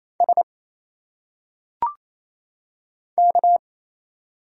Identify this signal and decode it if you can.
Morse code, heard as a keyed tone, about 28 wpm, 700 Hz. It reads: S K
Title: Single Letters - Phonetic Pronunciation 28wpm